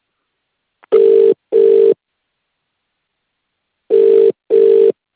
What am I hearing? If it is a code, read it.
Ringback tone (UK), 400 + 450 Hz.